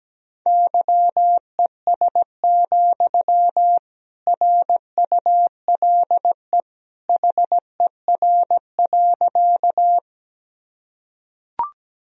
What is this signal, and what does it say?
Morse code, heard as a keyed tone, about 17 wpm, 700 Hz. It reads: YES, RULE HER.
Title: Sentences from Top 1000 Words - Review Part 2 - 17wpm